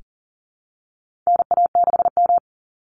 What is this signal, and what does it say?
Morse code, heard as a keyed tone, about 40 wpm, 700 Hz. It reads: NA6M
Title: Call Signs Repeated 3x - 40wpm